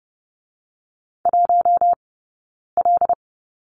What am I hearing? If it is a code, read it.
Morse code, heard as a keyed tone, about 30 wpm, spaced as for 9 wpm, 690 Hz. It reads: 1L
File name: course / 30:10wpm / 086 Lesson 15d - Number 5 - Two Characters 30wpm